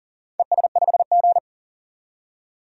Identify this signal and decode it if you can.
Morse code, heard as a keyed tone, about 38 wpm, 700 Hz. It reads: ES5G
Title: DXCC-100 Call Signs - 4 Characters - 40wpm